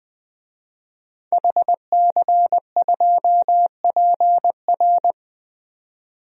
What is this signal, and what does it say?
Morse code, heard as a keyed tone, about 20 wpm, 700 Hz. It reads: HC2PR
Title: DXCC-100 Call Signs - 5 Characters - 20wpm